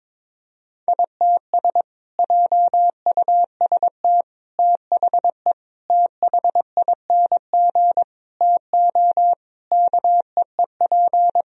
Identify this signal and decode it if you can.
Morse code, heard as a keyed tone, about 22 wpm, 700 Hz. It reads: ITS JUST THE THING TO KEEP
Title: Sentences from Top 1000 Words - Review Part 2 - 22wpm